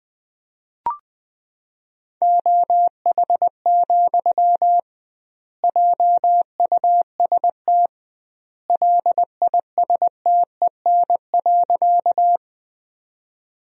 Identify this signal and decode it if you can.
Morse code, heard as a keyed tone, about 21 wpm, 700 Hz. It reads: OH, JUST LISTEN.
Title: 2x Word-Spacing - Sentences from Top 500 Words - 20wpm